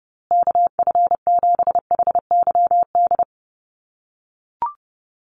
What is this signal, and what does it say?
Morse code, heard as a keyed tone, about 30 wpm, 700 Hz. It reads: KF7HYD
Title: USA Call Signs Repeated 2x 30wpm